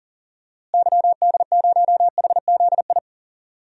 Morse code, heard as a keyed tone, about 39 wpm, 700 Hz. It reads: YD0HZI
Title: DXCC-100 Call Signs - 6 Characters - Repeated 3x - 40wpm